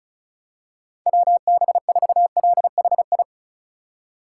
Morse code, heard as a keyed tone, about 36 wpm, 700 Hz. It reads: WB4LHI
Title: USA Call Signs - 6 Characters - 6x Character Spacing - 35wpm